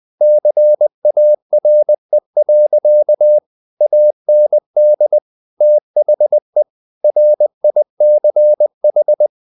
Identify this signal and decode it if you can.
Morse code, heard as a keyed tone, about 20 wpm, 600 Hz. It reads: CARE. AND THE RICH
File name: the old man and the sea / the old man and the sea part 1 20wpm